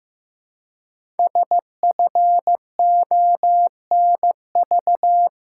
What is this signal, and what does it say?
Morse code, heard as a keyed tone, about 15 wpm, 700 Hz. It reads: SFONV
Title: Code Groups - 5 Letters - 15wpm